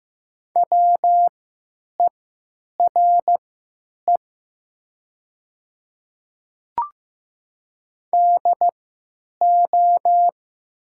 Morse code, heard as a keyed tone, about 15 wpm, spaced as for 8 wpm, 700 Hz. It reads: WERE DO
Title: 3x Character Spacing - Top 200 Words - 15wpm